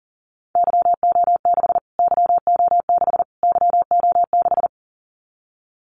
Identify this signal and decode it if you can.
Morse code, heard as a keyed tone, about 40 wpm, 700 Hz. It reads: YO6 YO6 YO6
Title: Callsign Prefixes for Top 25 DXCC - Repeated 3x - 40wpm